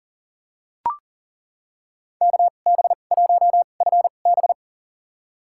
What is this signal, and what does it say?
Morse code, heard as a keyed tone, about 40 wpm, spaced as for 29 wpm, 700 Hz. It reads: KB1FB
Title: USA Call Signs - 5 Characters - 2x Character Spacing - 40wpm